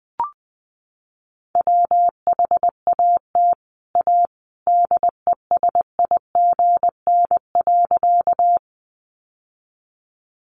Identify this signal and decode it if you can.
Morse code, heard as a keyed tone, about 20 wpm, 700 Hz. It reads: WHAT A DESIGN.
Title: Sentences from Top 800 Words 20wpm